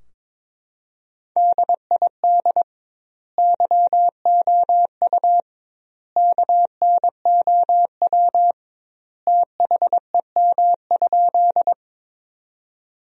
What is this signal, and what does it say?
Morse code, heard as a keyed tone, about 22 wpm, 700 Hz. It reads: DID YOU KNOW THEM?
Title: 2x Word-Spacing - Sentences from Top 100 Words - 22wpm